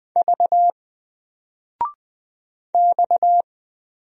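Morse code, heard as a keyed tone, about 20 wpm, 700 Hz. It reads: V X
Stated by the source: Single letter and number Morse drill at 20 wpm, from N4LCD.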